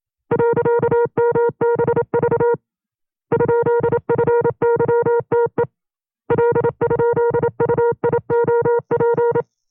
Morse code, heard as a keyed tone, about 27 wpm, 470 Hz. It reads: .MBV ?FYTE L?UIOP
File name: 25WPM